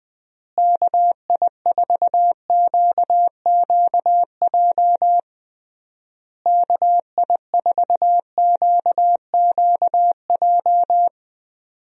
Morse code, heard as a keyed tone, about 20 wpm, 700 Hz. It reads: KI4QQJ KI4QQJ
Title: USA Call Signs - 6 Characters - Repeated 3x - 20wpm